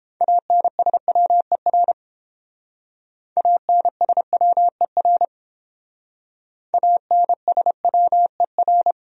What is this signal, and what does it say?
Morse code, heard as a keyed tone, about 31 wpm, 700 Hz. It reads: ANSWER ANSWER ANSWER